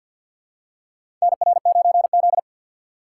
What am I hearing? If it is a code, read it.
Morse code, heard as a keyed tone, about 51 wpm, 700 Hz. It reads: NR9Z
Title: USA Call Signs - 4x Character Spacing - 50wpm